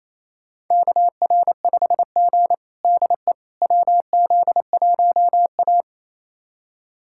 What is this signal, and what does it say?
Morse code, heard as a keyed tone, about 28 wpm, 700 Hz. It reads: KR5G DE WZ1A
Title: USA Callsign DE Callsign - 4 Characters - 28wpm